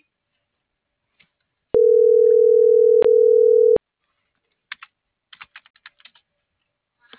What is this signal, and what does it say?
Ringback tone (North America), 440 + 480 Hz.